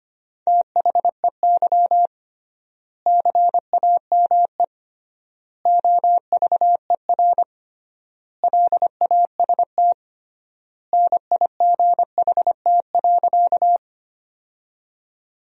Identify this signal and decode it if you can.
Morse code, heard as a keyed tone, about 25 wpm, 700 Hz. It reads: THEY CAME OVER LAST NIGHT.
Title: 3x Word-Spacing - Sentences from Top 300 Words - 25wpm